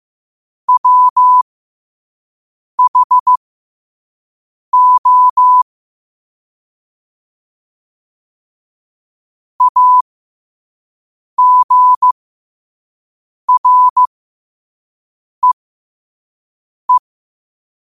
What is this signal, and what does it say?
Morse code, heard as a keyed tone, about 15 wpm, spaced as for 5 wpm, 1000 Hz. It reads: WHO AGREE
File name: qotdfiles / 5wpm